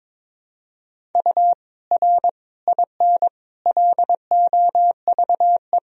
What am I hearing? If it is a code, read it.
Morse code, heard as a keyed tone, about 22 wpm, 700 Hz. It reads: U R IN LOVE
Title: Sentences from Top 400 Words and Abbreviations 22wpm